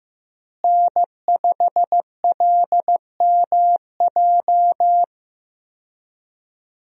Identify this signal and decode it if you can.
Morse code, heard as a keyed tone, about 16 wpm, 700 Hz. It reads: N5LMJ
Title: USA Call Signs - 5 Characters - Repeated 3x - 15wpm